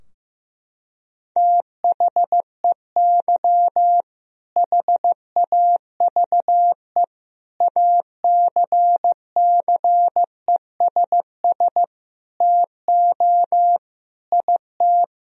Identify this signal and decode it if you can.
Morse code, heard as a keyed tone, about 16 wpm, 700 Hz. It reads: THEY HAVE ACCESS TO IT
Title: Sentences from Top 1000 Words - Review Part 4 - 15wpm